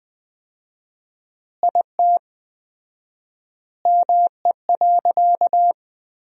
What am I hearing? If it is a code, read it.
Morse code, heard as a keyed tone, about 21 wpm, 700 Hz. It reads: IT ME.
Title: 4x Word-Spacing - Sentences from Top 200 Words - 20wpm